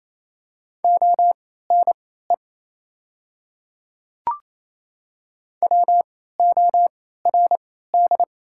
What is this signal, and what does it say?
Morse code, heard as a keyed tone, about 27 wpm, spaced as for 16 wpm, 690 Hz. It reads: ONE WORD
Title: 3x Character Spacing - Top 200 Words - 28wpm